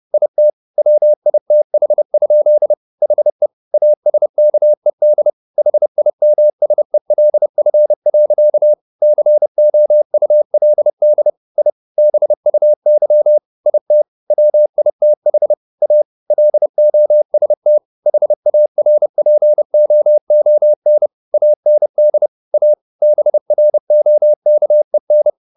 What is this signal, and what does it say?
Morse code, heard as a keyed tone, about 30 wpm, 600 Hz. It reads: IT WITH? HE ASKED HIMSELF. COULD I BUY IT WITH A LOST HARPOON AND A BROKEN